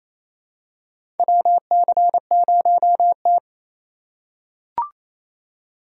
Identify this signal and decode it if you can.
Morse code, heard as a keyed tone, about 28 wpm, 700 Hz. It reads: WC0T
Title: USA Call Signs - 3x Character Spacing - 28wpm